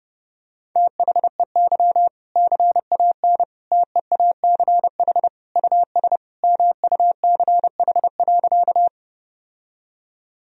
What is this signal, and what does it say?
Morse code, heard as a keyed tone, about 32 wpm, 700 Hz. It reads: THEY CAN TEACH US MUCH.
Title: Sentences from Top 1000 Words - Review Part 2 - 30wpm